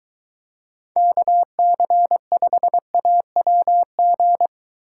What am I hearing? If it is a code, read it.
Morse code, heard as a keyed tone, about 22 wpm, 700 Hz. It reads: KC5AWG